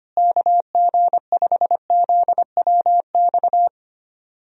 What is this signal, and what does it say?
Morse code, heard as a keyed tone, about 25 wpm, 700 Hz. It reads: KG5ZWX
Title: USA Call Signs - 6 Characters - Repeated 2x - 25wpm